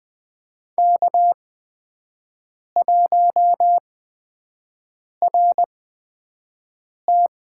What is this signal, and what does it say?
Morse code, heard as a keyed tone, about 21 wpm, spaced as for 5 wpm, 700 Hz. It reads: K1RT
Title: USA Call Signs - 5 Characters - 8x Character Spacing - 20wpm